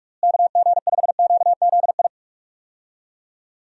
Morse code, heard as a keyed tone, about 46 wpm, 700 Hz. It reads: KG5QZI
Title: USA Call Signs - 6 Characters - Repeated 3x - 45wpm